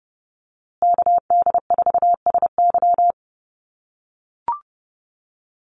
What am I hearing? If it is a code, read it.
Morse code, heard as a keyed tone, about 30 wpm, 700 Hz. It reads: KD4SY
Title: USA Call Signs Repeated 3x 30wpm